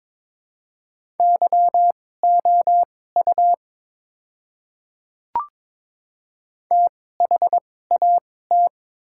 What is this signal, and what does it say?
Morse code, heard as a keyed tone, about 22 wpm, spaced as for 16 wpm, 700 Hz. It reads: YOU THAT
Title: 2x Character Spacing - Top 100 Words - 22wpm